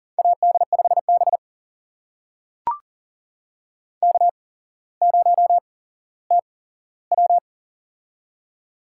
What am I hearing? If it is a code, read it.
Morse code, heard as a keyed tone, about 42 wpm, spaced as for 17 wpm, 700 Hz. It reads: AD5B K0TW
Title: USA Call Signs - 4 Characters - 8x Character Spacing - 40wpm